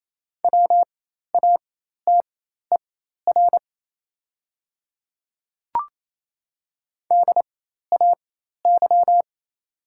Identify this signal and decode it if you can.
Morse code, heard as a keyed tone, about 28 wpm, spaced as for 13 wpm, 690 Hz. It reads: WATER DAY
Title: 4x Character Spacing - Top 100 Words - 28wpm